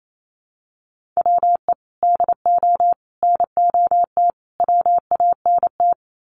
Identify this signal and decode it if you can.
Morse code, heard as a keyed tone, about 28 wpm, 700 Hz. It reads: WE DO NOT WANT
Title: Sets of 4 Words 28wpm